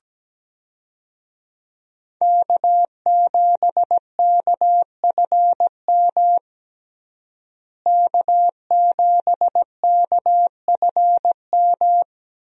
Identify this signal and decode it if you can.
Morse code, heard as a keyed tone, about 17 wpm, 700 Hz. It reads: K7KFM K7KFM
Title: USA Call Signs - 5 Characters - Repeated 2x - 17wpm